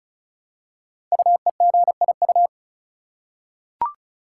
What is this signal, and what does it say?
Morse code, heard as a keyed tone, about 36 wpm, 700 Hz. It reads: UEGIU
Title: Code Groups - 5 Letters - 35wpm